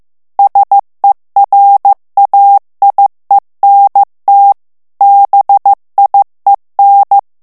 Morse code, heard as a keyed tone, about 15 wpm, 800 Hz. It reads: SERAIENT BIEN
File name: FRANCAIS TEXTE CLAIR 2 VITESSE 14 MOTS MINUTES